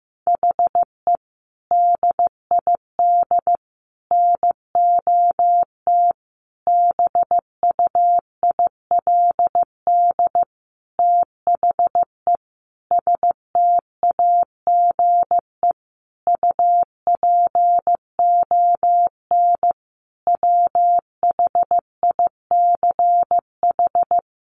Morse code, heard as a keyed tone, about 15 wpm, 700 Hz. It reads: HE DID NOT BUILD THE STAGE UPON WHICH
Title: Sentences from Top 1200 Words 15wpm